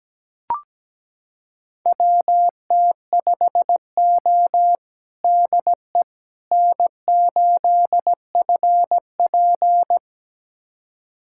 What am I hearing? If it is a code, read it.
Morse code, heard as a keyed tone, about 17 wpm, 700 Hz. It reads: WT5O DE N8FP
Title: USA Callsign DE Callsign - 4 Characters - 17wpm